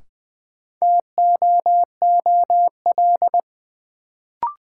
Morse code, heard as a keyed tone, about 20 wpm, 700 Hz. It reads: TOOL
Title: Sentences from Top 1000 Words - Review Part 6 - New Word Drill - 20wpm